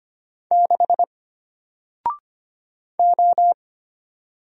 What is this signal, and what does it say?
Morse code, heard as a keyed tone, about 25 wpm, 700 Hz. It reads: 6 O
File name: Single-Letter-Number-25wpm